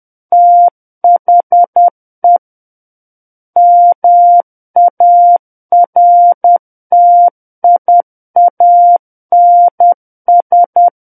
Morse code, heard as a keyed tone, about 10 wpm, 700 Hz. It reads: THE MARTIANS
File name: Book One / 01-Chapter 1 - 10 WPM